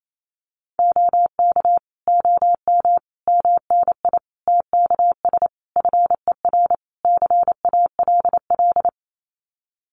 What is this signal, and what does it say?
Morse code, heard as a keyed tone, about 28 wpm, 700 Hz. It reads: OK OM MNI TKS FER CALL
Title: Common QSO Phrases 28wpm